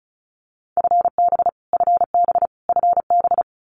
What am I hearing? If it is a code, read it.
Morse code, heard as a keyed tone, about 34 wpm, 700 Hz. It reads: FB FB FB
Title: Abbreviations Repeated 3x - 35wpm